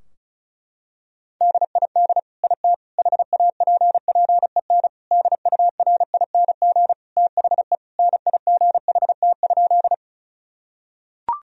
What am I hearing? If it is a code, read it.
Morse code, heard as a keyed tone, about 34 wpm, 700 Hz. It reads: DID IT HAPPEN DURING THE NIGHT?
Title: Sentences from Top 1000 Words - Review Part 2 - 35wpm